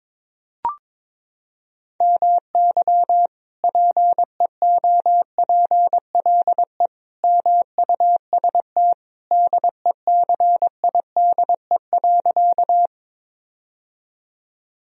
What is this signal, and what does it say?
Morse code, heard as a keyed tone, about 22 wpm, 700 Hz. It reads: MY PEOPLE MUST DECIDE.
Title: Sentences from Top 1000 Words - Review Part 3 - 22wpm